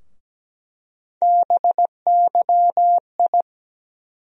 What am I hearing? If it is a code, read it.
Morse code, heard as a keyed tone, about 17 wpm, 700 Hz. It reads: BYI
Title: USA Call Sign Postfixes - 3 Letters 17wpm